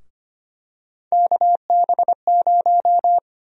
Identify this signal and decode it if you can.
Morse code, heard as a keyed tone, about 25 wpm, 700 Hz. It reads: KB0
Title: USA Call Sign Prefixes - Repeated 2x - 25wpm